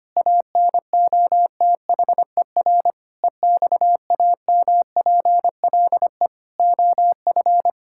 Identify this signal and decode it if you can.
Morse code, heard as a keyed tone, about 25 wpm, 700 Hz. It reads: ANOTHER EXAMPLE OF
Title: Sentences from Top 1000 Words - Review Part 5 - 25wpm